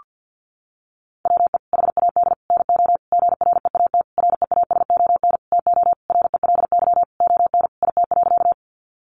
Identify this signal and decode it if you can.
Morse code, heard as a keyed tone, about 50 wpm, 700 Hz. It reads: WE HAD NO GREAT REASON TO RELY ON IT.
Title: Sentences from Top 1200 Words 50wpm